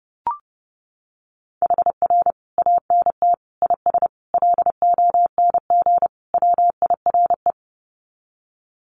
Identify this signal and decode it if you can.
Morse code, heard as a keyed tone, about 30 wpm, 700 Hz. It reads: HR ANT IS LONG WIRE
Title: Common QSO Phrases 30wpm